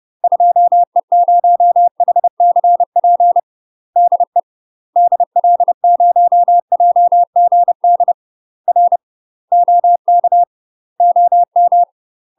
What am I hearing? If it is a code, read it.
Morse code, heard as a keyed tone, about 30 wpm, 700 Hz. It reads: 2E0HCP DE DL0JGD R OK OM
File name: UK MASTER QSOs TRAINING / 30WPM / UK MASTER QSOs 0035